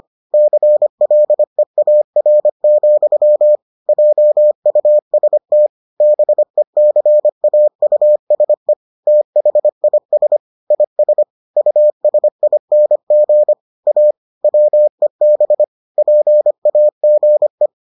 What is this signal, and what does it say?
Morse code, heard as a keyed tone, about 25 wpm, 600 Hz. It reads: CLEAR, JUST BECAUSE THIS IS USING A WEB PAGE